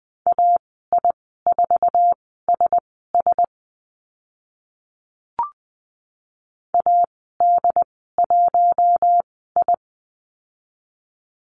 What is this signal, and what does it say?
Morse code, heard as a keyed tone, about 20 wpm, spaced as for 15 wpm, 700 Hz. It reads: AI4SS AD1I
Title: 069 Lesson 12g - Letter d - Call Signs 20wpm